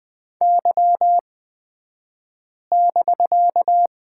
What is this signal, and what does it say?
Morse code, heard as a keyed tone, about 20 wpm, spaced as for 5 wpm, 700 Hz. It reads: Y<BK>
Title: 249 Lesson 40d - BK - Two Characters 20wpm